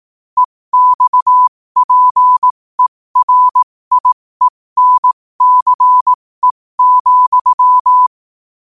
Morse code, heard as a keyed tone, about 18 wpm, spaced as for 15 wpm, 1000 Hz. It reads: EXPERIENCE,
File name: qotdfiles / 15wpm